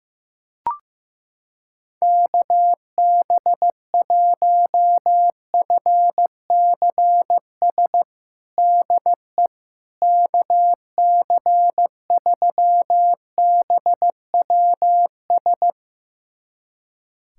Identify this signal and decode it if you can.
Morse code, heard as a keyed tone, about 16 wpm, 700 Hz. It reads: KB1FCS DE KC3BWS
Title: USA Callsign DE Callsign 15wpm